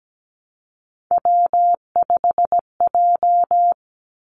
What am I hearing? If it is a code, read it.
Morse code, heard as a keyed tone, about 17 wpm, 700 Hz. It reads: W5J
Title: Three Challenging Letter-Number Combinations 17wpm